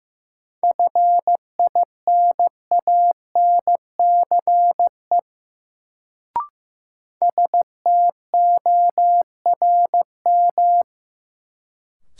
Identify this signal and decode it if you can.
Morse code, heard as a keyed tone, about 15 wpm, 700 Hz. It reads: FINANCE STORM
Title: Sentences from Top 1500 Words - New Word Drill - 15wpm